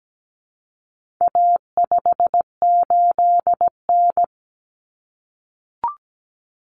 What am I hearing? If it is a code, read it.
Morse code, heard as a keyed tone, about 17 wpm, 700 Hz. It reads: A58N
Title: Common Cut Numbers - 4-Digit 17wpm